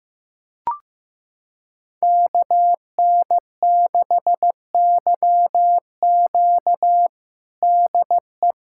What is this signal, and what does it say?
Morse code, heard as a keyed tone, about 16 wpm, 700 Hz. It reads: KN6YQ DE
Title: USA Callsign DE Callsign - 5 Characters - 15wpm